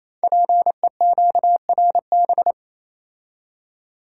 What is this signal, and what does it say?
Morse code, heard as a keyed tone, about 29 wpm, 690 Hz. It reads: PEQRB
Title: Code Groups - 5 Letters - 28wpm